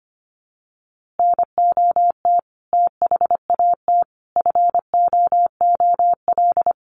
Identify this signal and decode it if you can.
Morse code, heard as a keyed tone, about 25 wpm, 700 Hz. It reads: NOT THAT FOOL
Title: A Sound Of Thunder 25wpm